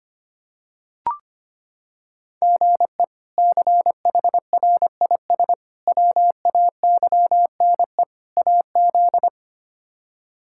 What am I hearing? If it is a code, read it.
Morse code, heard as a keyed tone, about 25 wpm, 700 Hz. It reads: GE CHRIS WAYNE AZ
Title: K1USN SST QSO Practice - 25wpm